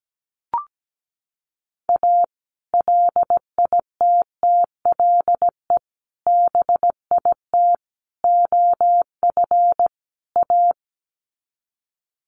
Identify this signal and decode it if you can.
Morse code, heard as a keyed tone, about 17 wpm, 700 Hz. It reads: A LITTLE BIT OF A
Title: Sets of 5 Words 17wpm